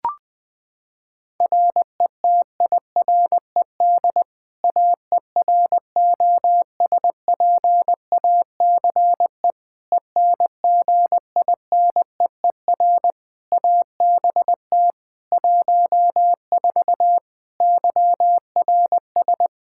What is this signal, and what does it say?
Morse code, heard as a keyed tone, about 21 wpm, 700 Hz. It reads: RETIRED AEROSPACE ENGINEER ABT 14 YRS
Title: K1USN SST QSO Practice 20wpm